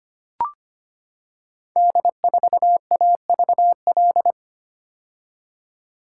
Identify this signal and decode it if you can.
Morse code, heard as a keyed tone, about 25 wpm, 700 Hz. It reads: D4AVL
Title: Code Groups - 5 Letter:Numbers - 25wpm